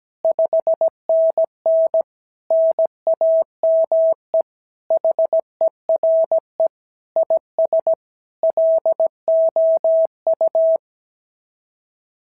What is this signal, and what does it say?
Morse code, heard as a keyed tone, about 16 wpm, 650 Hz. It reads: 5NN NAME HERE IS LOU
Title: SOTA QSO Practice 17wpm